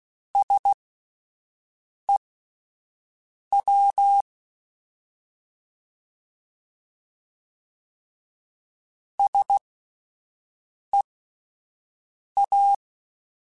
Morse code, heard as a keyed tone, about 16 wpm, spaced as for 5 wpm, 800 Hz. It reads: SEW SEA